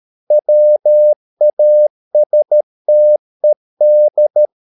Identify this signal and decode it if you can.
Morse code, heard as a keyed tone, about 13 wpm, 600 Hz. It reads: WASTED